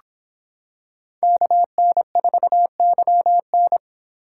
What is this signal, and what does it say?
Morse code, heard as a keyed tone, about 25 wpm, 700 Hz. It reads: KN4YN